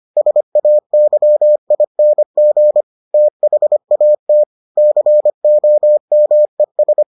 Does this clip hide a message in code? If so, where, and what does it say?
Yes: Morse code from 0.2 to 7.0 s, heard as a keyed tone, about 25 wpm, 600 Hz. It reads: SAYING THAT COMES